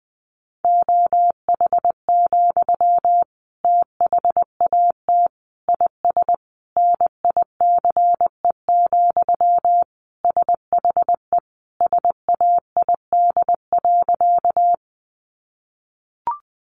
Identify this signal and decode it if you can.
Morse code, heard as a keyed tone, about 20 wpm, 700 Hz. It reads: OH, THAT IS NICE, SHE SAID.
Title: Sentences from Top 1100 Words 20wpm